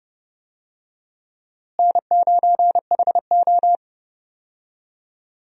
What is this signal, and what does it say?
Morse code, heard as a keyed tone, about 32 wpm, 700 Hz. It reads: N9HO
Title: USA Call Signs - 4 Characters - Repeated 2x - 30wpm